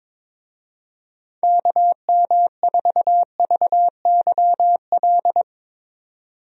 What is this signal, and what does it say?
Morse code, heard as a keyed tone, about 22 wpm, 700 Hz. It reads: KM4VYL
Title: USA Call Signs - 6 Characters - 22wpm